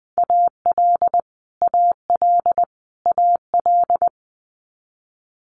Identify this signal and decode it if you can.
Morse code, heard as a keyed tone, about 20 wpm, 700 Hz. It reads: AL AL AL
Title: Multi-letter Phonemes Repeated 3x 20wpm